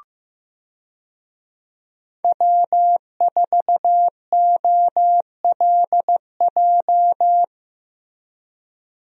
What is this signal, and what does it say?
Morse code, heard as a keyed tone, about 15 wpm, 700 Hz. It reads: W4OLJ
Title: USA Call Signs - 5 Characters - Repeated 2x - 15wpm